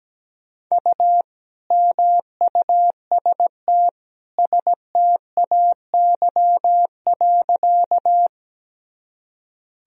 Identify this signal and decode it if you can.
Morse code, heard as a keyed tone, about 17 wpm, 700 Hz. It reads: U MUST STAY.
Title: Sentences from Top 500 Words and Abbreviations 17wpm